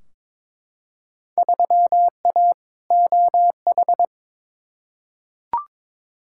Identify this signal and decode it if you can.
Morse code, heard as a keyed tone, about 23 wpm, 700 Hz. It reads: 3A OH
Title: Field Day Exchanges - 22wpm